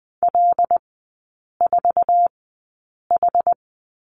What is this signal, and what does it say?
Morse code, heard as a keyed tone, about 20 wpm, spaced as for 8 wpm, 690 Hz. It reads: L4H
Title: 081 Lesson 14e - Number 2 - Three Characters 20wpm